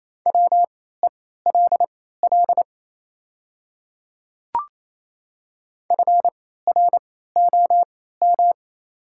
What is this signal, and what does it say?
Morse code, heard as a keyed tone, about 28 wpm, spaced as for 16 wpm, 700 Hz. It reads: WELL FROM
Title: 3x Character Spacing - Top 300 Words - 28wpm